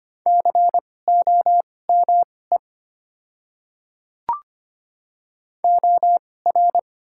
Morse code, heard as a keyed tone, about 25 wpm, spaced as for 18 wpm, 700 Hz. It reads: COME OR
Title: 2x Character Spacing - Top 100 Words - 25wpm